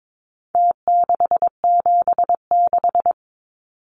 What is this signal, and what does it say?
Morse code, heard as a keyed tone, about 22 wpm, 700 Hz. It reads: T676
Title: Numbers - Contest Style - 4-Digit 22wpm